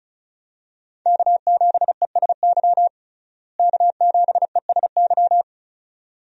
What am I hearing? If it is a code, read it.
Morse code, heard as a keyed tone, about 34 wpm, 700 Hz. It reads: K7ESY K7ESY
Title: USA Call Signs - 5 Characters - Repeated 2x - 35wpm